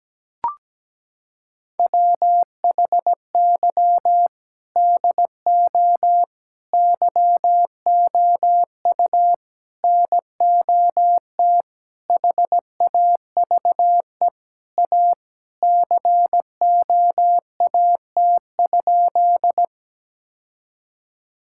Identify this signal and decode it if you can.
Morse code, heard as a keyed tone, about 17 wpm, 700 Hz. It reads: WHY DO YOU NOT HAVE A COAT?
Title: Sentences from Top 1600 Words 17wpm